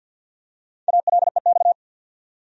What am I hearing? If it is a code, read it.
Morse code, heard as a keyed tone, about 51 wpm, 700 Hz. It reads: ALEX
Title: USA Common Names - 50wpm